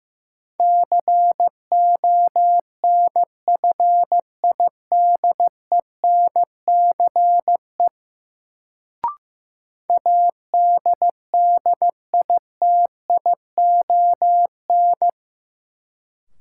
Morse code, heard as a keyed tone, about 15 wpm, 700 Hz. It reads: CONFIDENCE ADDITION
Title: Sentences from Top 1500 Words - New Word Drill Alt - 15wpm